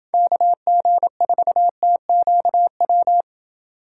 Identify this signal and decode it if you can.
Morse code, heard as a keyed tone, about 26 wpm, 700 Hz. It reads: KG4TQW